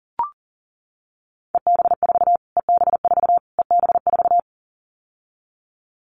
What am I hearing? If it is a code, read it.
Morse code, heard as a keyed tone, about 40 wpm, 700 Hz. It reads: EB4 EB4 EB4
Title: Callsign Prefixes for Top 25 DXCC - Repeated 3x - 40wpm